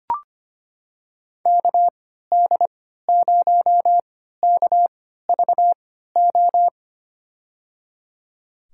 Morse code, heard as a keyed tone, about 25 wpm, spaced as for 14 wpm, 690 Hz. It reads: KD0KVO
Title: USA Call Signs - 6 Characters - 3x Character Spacing - 25wpm